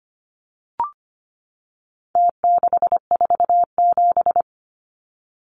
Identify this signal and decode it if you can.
Morse code, heard as a keyed tone, about 25 wpm, 700 Hz. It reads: T647
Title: Numbers - Contest Style - 4-Digit 25wpm